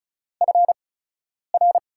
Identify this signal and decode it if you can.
Morse code, heard as a keyed tone, about 36 wpm, spaced as for 10 wpm, 700 Hz. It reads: FR
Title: USA Call Signs - 5 Characters - 8x Character Spacing - 35wpm